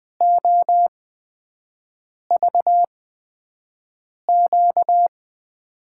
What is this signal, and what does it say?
Morse code, heard as a keyed tone, about 20 wpm, spaced as for 5 wpm, 700 Hz. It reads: OVQ
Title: USA Call Signs - 8x Character Spacing - 20wpm